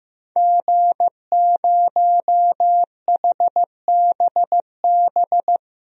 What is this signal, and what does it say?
Morse code, heard as a keyed tone, about 15 wpm, 700 Hz. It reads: G0HBB
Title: DXCC-100 Call Signs - 5 Characters - 15wpm